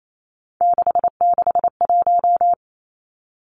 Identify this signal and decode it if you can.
Morse code, heard as a keyed tone, about 28 wpm, 700 Hz. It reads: 661
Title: Three Challenging Letter-Number Combinations 28wpm